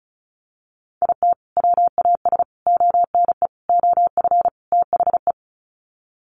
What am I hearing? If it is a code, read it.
Morse code, heard as a keyed tone, about 34 wpm, 700 Hz. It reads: IT WAS ONE OF THE
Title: Sets of 5 Words 35wpm